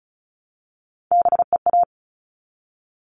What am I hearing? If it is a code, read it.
Morse code, heard as a keyed tone, about 34 wpm, 700 Hz. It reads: BEA
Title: All Cut Numbers - 3-Digit 35wpm